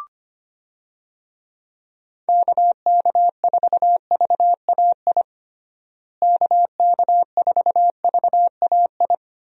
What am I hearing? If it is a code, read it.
Morse code, heard as a keyed tone, about 25 wpm, 700 Hz. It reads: KK4VAI KK4VAI
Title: USA Call Signs - 6 Characters - Repeated 2x - 25wpm